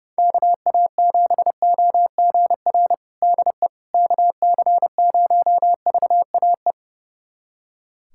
Morse code, heard as a keyed tone, about 30 wpm, 700 Hz. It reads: KA7OGR DE KC0VAE
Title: USA Callsign DE Callsign - 6 Characters - 30wpm